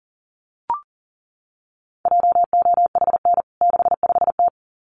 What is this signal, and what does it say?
Morse code, heard as a keyed tone, about 40 wpm, 700 Hz. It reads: JOHN 65T
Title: CWT Exchange 40wpm